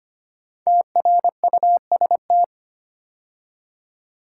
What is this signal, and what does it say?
Morse code, heard as a keyed tone, about 25 wpm, 700 Hz. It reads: TRUST
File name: 5 Letter Words - First 3 Letters Given - 25wpm